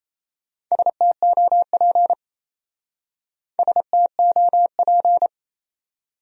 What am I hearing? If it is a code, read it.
Morse code, heard as a keyed tone, about 33 wpm, 700 Hz. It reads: STOP STOP
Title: Speed-Racing Top 300 Words 28wpm